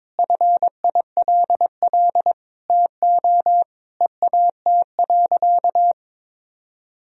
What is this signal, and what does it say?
Morse code, heard as a keyed tone, about 21 wpm, 700 Hz. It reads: FILL TO EAT.
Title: Sentences from Top 1000 Words - Review Part 3 - 22wpm